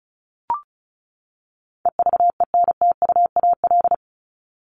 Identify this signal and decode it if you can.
Morse code, heard as a keyed tone, about 34 wpm, 700 Hz. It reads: EVENTUAL
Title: LY Suffix - Top 100 Words with Base Words 35wpm